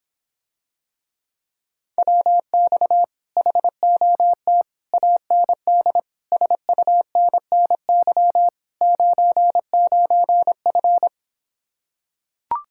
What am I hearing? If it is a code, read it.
Morse code, heard as a keyed tone, about 26 wpm, 700 Hz. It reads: WX HOT AND SUNNY 99F